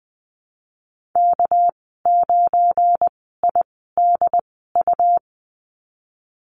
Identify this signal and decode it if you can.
Morse code, heard as a keyed tone, about 20 wpm, spaced as for 15 wpm, 700 Hz. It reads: K9IDU
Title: 219 Lesson 35g - Number 0 - Call Signs 20wpm